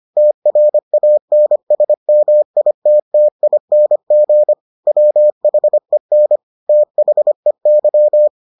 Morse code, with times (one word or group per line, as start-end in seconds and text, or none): 0.2-4.5 TRANSMITTING
4.9-6.4 WHEN
6.7-8.3 THEY